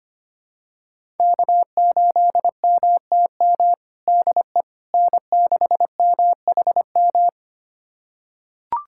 Morse code, heard as a keyed tone, about 25 wpm, 700 Hz. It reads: K8MTM DE N6MHM
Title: USA Callsign DE Callsign - 5 Characters - 25wpm